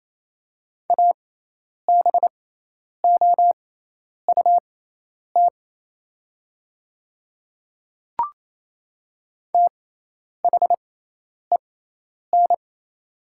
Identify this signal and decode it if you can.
Morse code, heard as a keyed tone, about 27 wpm, spaced as for 10 wpm, 700 Hz. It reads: ABOUT THEN
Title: 6x Character Spacing - Top 100 Words - 28wpm